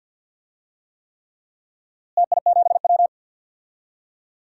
Morse code, heard as a keyed tone, about 50 wpm, 700 Hz. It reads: TI7W
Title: DXCC-100 Call Signs - 4 Characters - Repeated 3x - 50wpm